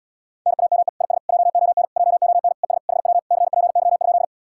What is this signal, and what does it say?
Morse code, heard as a keyed tone, about 78 wpm, 700 Hz. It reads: NAME IS JON JON ES HW CPY?